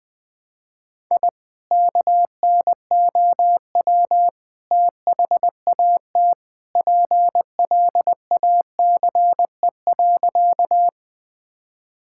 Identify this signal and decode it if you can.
Morse code, heard as a keyed tone, about 20 wpm, 700 Hz. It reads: I KNOW THAT PLACE.
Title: Sentences with 2-Word Phrases - 20wpm